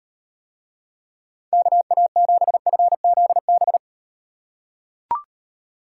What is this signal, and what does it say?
Morse code, heard as a keyed tone, about 38 wpm, 700 Hz. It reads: KA7FZB